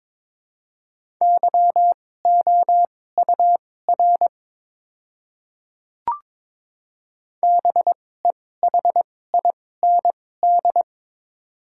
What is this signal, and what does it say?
Morse code, heard as a keyed tone, about 23 wpm, spaced as for 16 wpm, 700 Hz. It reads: YOUR BEHIND
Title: 2x Character Spacing - Top 500 Words - 22wpm